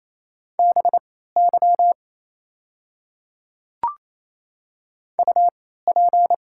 Morse code, heard as a keyed tone, about 27 wpm, spaced as for 16 wpm, 690 Hz. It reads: BY UP
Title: 3x Character Spacing - Top 200 Words - 28wpm